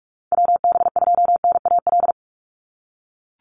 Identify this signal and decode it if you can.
Morse code, heard as a keyed tone, about 44 wpm, 700 Hz. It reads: WB2NAL
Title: USA Call Signs 45wpm